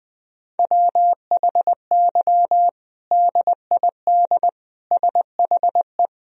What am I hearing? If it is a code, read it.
Morse code, heard as a keyed tone, about 20 wpm, 700 Hz. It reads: WHY DID SHE
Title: Sentences from Top 1000 Words - Review Part 3 - 20wpm